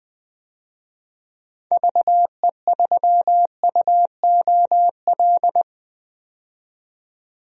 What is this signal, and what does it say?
Morse code, heard as a keyed tone, about 21 wpm, 700 Hz. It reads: VE3UOL
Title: DXCC-100 Call Signs - 6 Characters - 20wpm